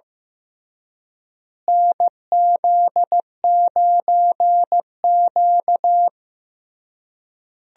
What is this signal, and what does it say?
Morse code, heard as a keyed tone, about 15 wpm, 700 Hz. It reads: NZ9Q
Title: USA Call Signs - 4 Characters - Repeated 3x - 15wpm